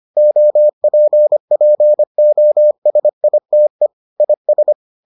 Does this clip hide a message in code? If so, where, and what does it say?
Yes: Morse code from 0.2 to 4.7 s, heard as a keyed tone, about 25 wpm, 600 Hz. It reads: OPPOSITE IS